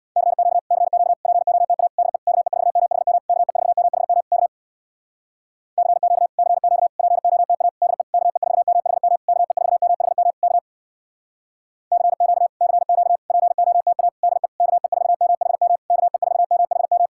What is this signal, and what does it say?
Morse code, heard as a keyed tone, about 61 wpm, 700 Hz. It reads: CQ CQ POTA DE KE4MHM KE4MHM K CQ CQ POTA DE KE4MHM KE4MHM K CQ CQ POTA DE KE4MHM KE4MHM
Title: Speed-Racing Common QSO Phrases Encore - 50wpm